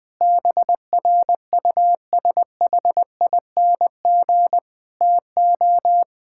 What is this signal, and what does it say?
Morse code, heard as a keyed tone, about 20 wpm, 700 Hz. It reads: BRUSHING TO
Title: Difficult Sentences with Hints - 20wpm